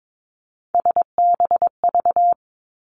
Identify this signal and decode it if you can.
Morse code, heard as a keyed tone, about 22 wpm, 700 Hz. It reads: SBV
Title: Three Challenging Letter-Number Combinations 22wpm